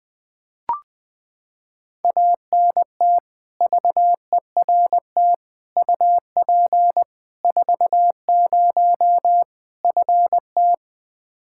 Morse code, heard as a keyed tone, about 20 wpm, 700 Hz. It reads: ANT VERT UP 40 FT
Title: Practice QSO - Exchange 2 - Info - 20wpm